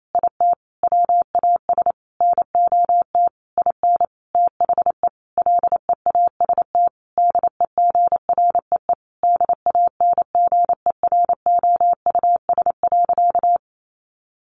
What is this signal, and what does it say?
Morse code, heard as a keyed tone, about 28 wpm, 700 Hz. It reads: IT WAS NOT IN THE LEAST DEGREE DANGEROUS.
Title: Sentences from Top 1500 Words 28wpm